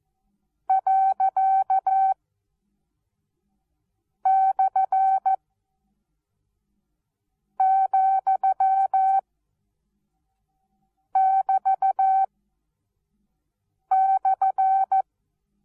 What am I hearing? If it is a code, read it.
Morse code, heard as a keyed tone, about 14 wpm, spaced as for 4 wpm, 770 Hz. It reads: ./,<BT>/
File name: Onlinelehrgang / morsen / DJ4UF Morsen19